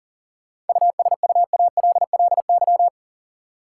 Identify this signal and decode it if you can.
Morse code, heard as a keyed tone, about 40 wpm, 700 Hz. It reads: USUALLY
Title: Sentences from Top 1000 Words - Review Part 3 - New Word Drill - 40wpm